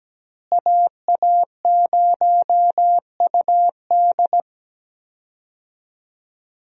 Morse code, heard as a keyed tone, about 17 wpm, 700 Hz. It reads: AA0UD
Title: USA Call Signs - 5 Characters - Repeated 3x - 17wpm